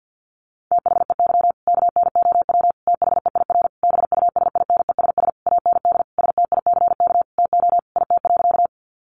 Morse code, heard as a keyed tone, about 50 wpm, 700 Hz. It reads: THEY KNOW THEIR BUSINESS AND STICK TO IT.
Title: Sentences from Top 1200 Words 50wpm